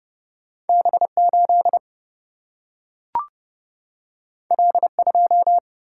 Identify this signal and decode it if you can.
Morse code, heard as a keyed tone, about 30 wpm, 700 Hz. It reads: B8 L2
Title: Single-Letter Single-Number 30wpm